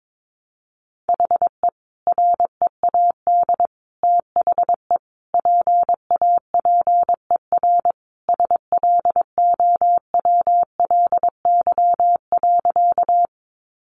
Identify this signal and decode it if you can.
Morse code, heard as a keyed tone, about 22 wpm, 700 Hz. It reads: HE READ THE PAPER SLOWLY.
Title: Sentences from Top 1100 Words 22wpm